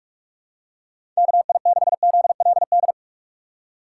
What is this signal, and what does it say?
Morse code, heard as a keyed tone, about 46 wpm, 700 Hz. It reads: KI6ZLD
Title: USA Call Signs - 6 Characters - 6x Character Spacing - 45wpm